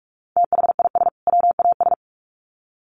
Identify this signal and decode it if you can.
Morse code, heard as a keyed tone, about 44 wpm, 700 Hz. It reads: THIS WAS
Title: Sets of 2 Words 45wpm